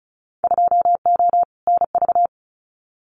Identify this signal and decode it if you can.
Morse code, heard as a keyed tone, about 34 wpm, 690 Hz. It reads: 2O NV
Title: Winter Field Day Exchanges 35wpm